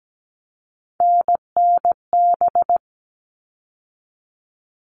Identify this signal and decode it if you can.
Morse code, heard as a keyed tone, about 17 wpm, 700 Hz. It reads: NNB
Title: All Cut Numbers - 3-Digit 17wpm